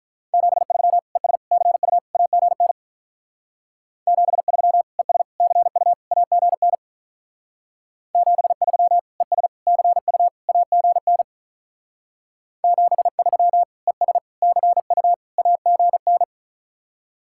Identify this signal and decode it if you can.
Morse code, heard as a keyed tone, about 43 wpm, 700 Hz. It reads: 73 ES CU AGN 73 ES CU AGN 73 ES CU AGN 73 ES CU AGN
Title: Speed-Racing Common QSO Phrases - 35wpm